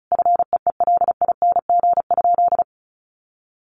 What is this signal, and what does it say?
Morse code, heard as a keyed tone, about 34 wpm, 700 Hz. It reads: FEELING?
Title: Sentences from Top 1200 Words 35wpm